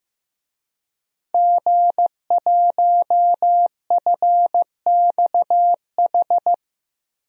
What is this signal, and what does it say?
Morse code, heard as a keyed tone, about 15 wpm, 700 Hz. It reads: G1FXH
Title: DXCC-100 Call Signs - 5 Characters - 15wpm